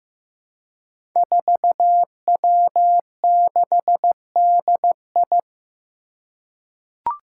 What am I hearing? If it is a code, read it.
Morse code, heard as a keyed tone, about 16 wpm, 700 Hz. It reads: 4W6DI
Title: Five Challenging Letter-Number Combinations 15wpm